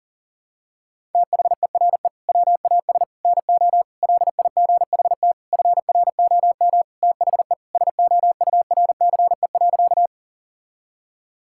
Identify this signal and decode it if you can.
Morse code, heard as a keyed tone, about 40 wpm, 700 Hz. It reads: THERE WAS NO LIGHT FROM THE SOURCE.
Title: Sentences from Top 1000 Words - Review Part 5 - 40wpm